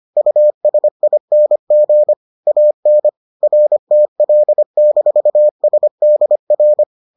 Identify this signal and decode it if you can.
Morse code, heard as a keyed tone, about 25 wpm, 600 Hz. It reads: USING AN RTL-SDR